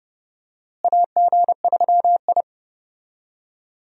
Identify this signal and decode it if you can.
Morse code, heard as a keyed tone, about 30 wpm, 700 Hz. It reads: AG3I
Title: USA Call Signs - 4 Characters - Repeated 2x - 30wpm